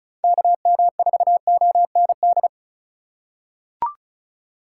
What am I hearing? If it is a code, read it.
Morse code, heard as a keyed tone, about 34 wpm, 700 Hz. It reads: KM4OND
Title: USA Call Signs - 6 Characters - 4x Character Spacing - 35wpm